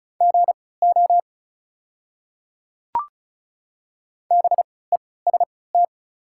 Morse code, heard as a keyed tone, about 36 wpm, spaced as for 20 wpm, 690 Hz. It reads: GO BEST
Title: 3x Character Spacing - Top 500 Words - 35wpm